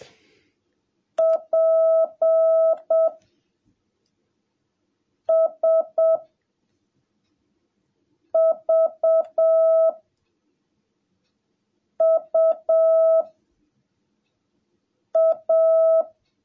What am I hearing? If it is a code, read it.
Morse code, heard as a keyed tone, about 7 wpm, 660 Hz. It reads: P S V U A
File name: jh8cbh / 2021cw / cw17a